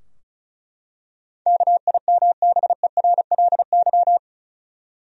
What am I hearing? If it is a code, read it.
Morse code, heard as a keyed tone, about 34 wpm, 700 Hz. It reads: KIMBERLY
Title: USA Common Names - 35wpm